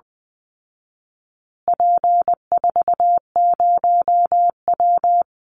Morse code, heard as a keyed tone, about 20 wpm, 700 Hz. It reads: P40W
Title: Call Signs Repeated 2x - 20wpm